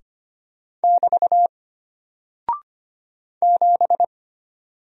Morse code, heard as a keyed tone, about 25 wpm, 700 Hz. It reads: <BT> 7
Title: Single Character (Letter:Number:Punctuation:Prosign) Alt 25wpm